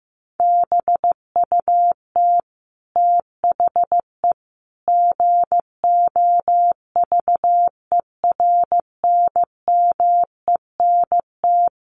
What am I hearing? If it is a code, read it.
Morse code, heard as a keyed tone, about 15 wpm, 700 Hz. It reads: BUT THE GOVERNMENT
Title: Sentences from Top 1400 Words 15wpm